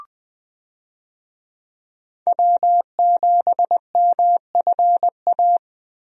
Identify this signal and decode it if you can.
Morse code, heard as a keyed tone, about 20 wpm, 700 Hz. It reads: W7MFA
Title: USA Call Signs - 5 Characters - Repeated 2x - 20wpm